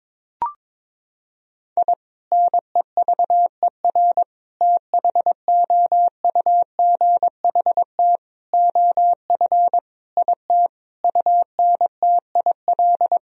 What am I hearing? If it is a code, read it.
Morse code, heard as a keyed tone, about 22 wpm, 700 Hz. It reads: I NEVER THOUGHT OF IT UNTIL
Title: Sentences from Top 1000 Words - Review Part 1 - 22wpm